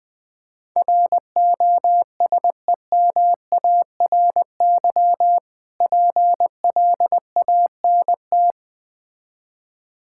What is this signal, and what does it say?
Morse code, heard as a keyed tone, about 20 wpm, 700 Hz. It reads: ROSEMARY PLANT
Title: Difficult Words with Hints - 20wpm